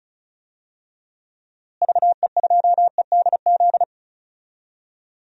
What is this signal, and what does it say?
Morse code, heard as a keyed tone, about 34 wpm, 700 Hz. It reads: VE2EDZ
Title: DXCC-100 Call Signs - 6 Characters - Repeated 3x - 35wpm